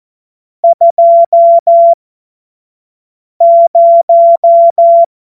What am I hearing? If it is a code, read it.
Morse code, heard as a keyed tone, about 14 wpm, spaced as for 5 wpm, 680 Hz. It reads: 20
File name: Morsekurs / Witze / Witz 26. April 2022 25 F70